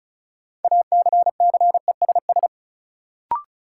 Morse code, heard as a keyed tone, about 36 wpm, 700 Hz. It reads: ACCESS
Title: Sentences from Top 1000 Words - Review Part 4 - New Word Drill - 35wpm